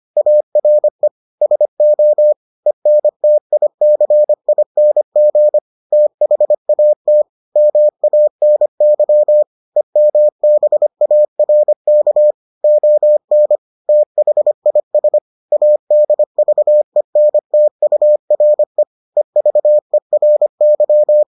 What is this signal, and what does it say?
Morse code, heard as a keyed tone, about 25 wpm, 600 Hz. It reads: ARE SO ENTICING THAT MANY EMBARK ON THIS ADVENTURE EVERY